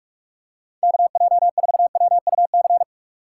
Morse code, heard as a keyed tone, about 46 wpm, 700 Hz. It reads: KJ4WUC
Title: USA Call Signs - 6 Characters - 3x Character Spacing - 45wpm